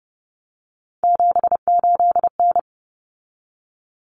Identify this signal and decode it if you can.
Morse code, heard as a keyed tone, about 30 wpm, 700 Hz. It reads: 78N
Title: Common Cut Numbers - 3-Digit 30wpm